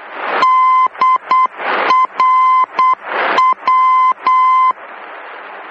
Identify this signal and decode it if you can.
Morse code, heard as a keyed tone, about 8 wpm, 1030 Hz. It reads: DRW
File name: germany / mx / DRW-D